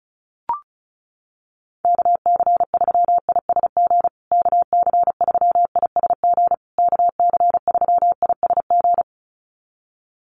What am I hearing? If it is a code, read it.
Morse code, heard as a keyed tone, about 34 wpm, 700 Hz. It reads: KC3ISG KC3ISG KC3ISG
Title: USA Call Signs Repeated 3x 35wpm